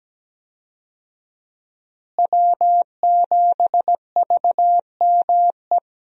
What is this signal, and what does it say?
Morse code, heard as a keyed tone, about 17 wpm, 700 Hz. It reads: W7VME
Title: USA Call Signs - 5 Characters - Repeated 3x - 17wpm